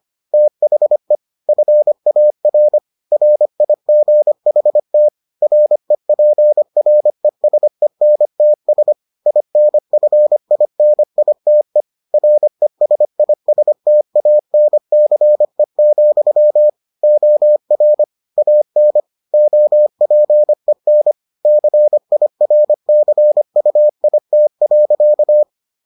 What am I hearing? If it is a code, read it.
Morse code, heard as a keyed tone, about 25 wpm, 600 Hz. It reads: THE FAR RIGHT REPRESENTS INFINITE RESISTANCE, OR AN OPEN CIRCUIT.